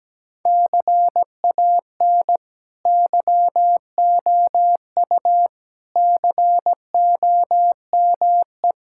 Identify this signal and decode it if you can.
Morse code, heard as a keyed tone, about 17 wpm, 700 Hz. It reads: CAN YOU COME
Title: Sentences from Top 1000 Words - Review Part 5 - 17wpm